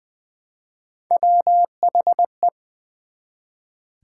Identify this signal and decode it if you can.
Morse code, heard as a keyed tone, about 20 wpm, 700 Hz. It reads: WHE
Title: Build A Word - 1 Letter At A Time - 3 to 5-Letter Words - 20wpm